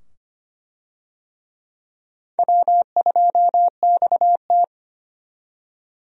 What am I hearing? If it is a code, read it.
Morse code, heard as a keyed tone, about 25 wpm, 700 Hz. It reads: W2XT
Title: USA Call Signs - 4 Characters - 25wpm